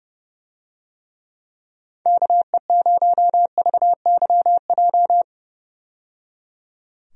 Morse code, heard as a keyed tone, about 31 wpm, 700 Hz. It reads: KE0VYJ
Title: USA Call Signs - 6 Characters - 30wpm